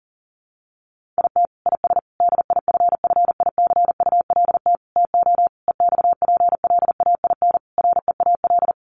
Morse code, heard as a keyed tone, about 40 wpm, 700 Hz. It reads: IT IS DIFFICULT TO EXPLAIN REAL
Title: Sentences from Top 1100 Words 40wpm